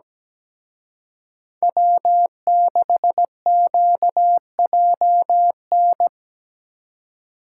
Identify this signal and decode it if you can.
Morse code, heard as a keyed tone, about 19 wpm, 700 Hz. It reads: W6QJN
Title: Speed Racing - USA 1x3 Call Signs 17wpm